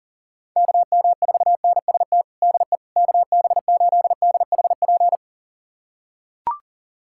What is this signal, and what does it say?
Morse code, heard as a keyed tone, about 37 wpm, 700 Hz. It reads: KM4NST DE KB8DHP
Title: USA Callsign DE Callsign 40wpm